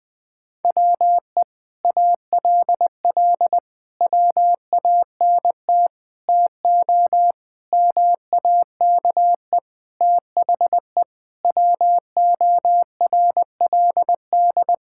Morse code, heard as a keyed tone, about 20 wpm, 700 Hz. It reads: WE ALL WANT TO MAKE THE WORLD
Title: Sentences with 7-Word Phrases - Spoken Beginning - 20wpm